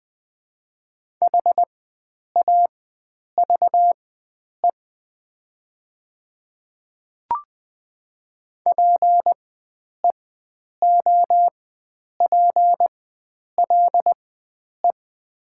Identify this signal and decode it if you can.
Morse code, heard as a keyed tone, about 20 wpm, spaced as for 9 wpm, 700 Hz. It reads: HAVE PEOPLE
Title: 4x Character Spacing - Top 100 Words - 20wpm